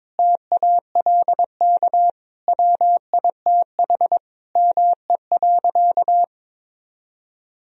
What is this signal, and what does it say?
Morse code, heard as a keyed tone, about 22 wpm, 700 Hz. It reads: TALK WITH ME.
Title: Sentences from Top 1000 Words - Review Part 3 - 22wpm